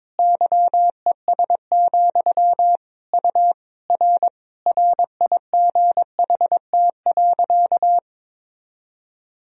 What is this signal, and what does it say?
Morse code, heard as a keyed tone, about 22 wpm, 700 Hz. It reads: YES, U R RIGHT.
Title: Sentences from Top 500 Words and Abbreviations 22wpm